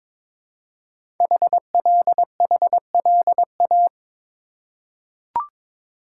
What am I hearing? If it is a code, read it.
Morse code, heard as a keyed tone, about 22 wpm, 700 Hz. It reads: HLHLA
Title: Code Groups - 5 Letters - 22wpm